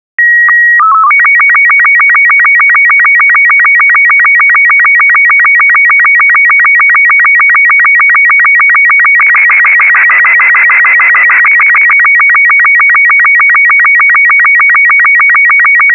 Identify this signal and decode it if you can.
SSTV (slow-scan television), Robot 36.